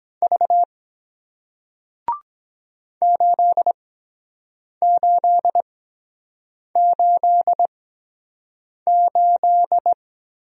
Morse code, heard as a keyed tone, about 22 wpm, spaced as for 7 wpm, 700 Hz. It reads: V 8888